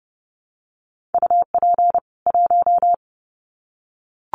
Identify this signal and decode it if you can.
Morse code, heard as a keyed tone, about 29 wpm, 700 Hz. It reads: UP 1
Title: Common QSO Phrases 30wpm